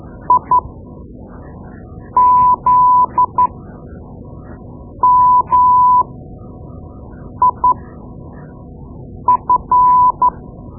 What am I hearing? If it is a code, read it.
Morse code, heard as a keyed tone, about 12 wpm, spaced as for 5 wpm, 1000 Hz. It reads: IZMIF
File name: Morse Lessons / Morse Lessons / Tape-2B